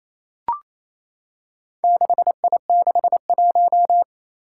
Morse code, heard as a keyed tone, about 27 wpm, 700 Hz. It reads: 6I61
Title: Four Challenging Letter-Number Combinations 28wpm